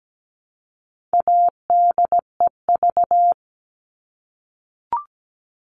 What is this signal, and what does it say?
Morse code, heard as a keyed tone, about 17 wpm, 700 Hz. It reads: ADEV
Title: All Cut Numbers - 4-Digit 17wpm